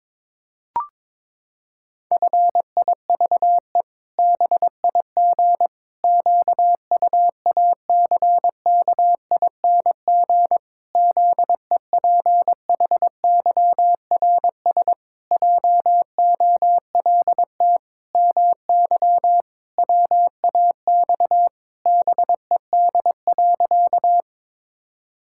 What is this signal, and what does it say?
Morse code, heard as a keyed tone, about 22 wpm, 700 Hz. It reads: FIVE BIG QUACKING ZEPHYRS JOLT MY WAX BED.
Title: Pangram Challenge v2 - 22wpm